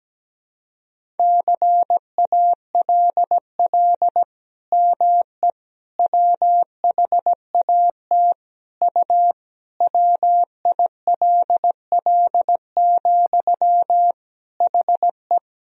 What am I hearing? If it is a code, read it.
Morse code, heard as a keyed tone, about 17 wpm, 700 Hz. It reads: CALL ME WHAT U WILL, HE
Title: Sentences from Top 200 Words and Abbreviations 17wpm